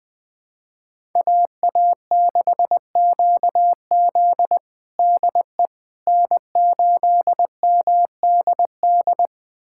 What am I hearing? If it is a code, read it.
Morse code, heard as a keyed tone, about 20 wpm, 700 Hz. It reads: AA6QZ DE N8MDD
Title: USA Callsign DE Callsign - 5 Characters - 20wpm